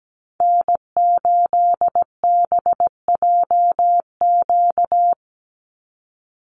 Morse code, heard as a keyed tone, about 17 wpm, 700 Hz. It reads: N8BJQ
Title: Call Signs - 17wpm